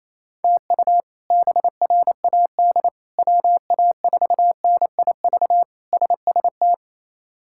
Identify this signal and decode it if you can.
Morse code, heard as a keyed tone, about 27 wpm, 700 Hz. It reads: TU BRAD WA4NIV SST
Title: K1USN SST QSO Practice - 28wpm